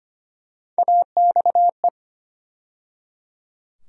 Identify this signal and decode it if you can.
Morse code, heard as a keyed tone, about 25 wpm, 700 Hz. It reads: AXE
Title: USA Call Sign Postfixes - 3 Letters 25wpm